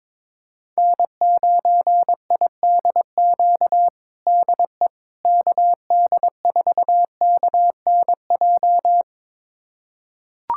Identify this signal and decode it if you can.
Morse code, heard as a keyed tone, about 22 wpm, 700 Hz. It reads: N9IDQ DE KD4KNJ
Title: USA Callsign DE Callsign 22wpm